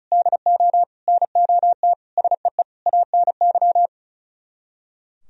Morse code, heard as a keyed tone, about 44 wpm, 700 Hz. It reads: DO NOT SEE ANY